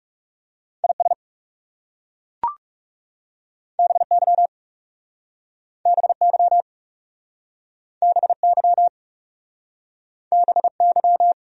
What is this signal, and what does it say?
Morse code, heard as a keyed tone, about 40 wpm, 700 Hz. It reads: IS BY BY BY BY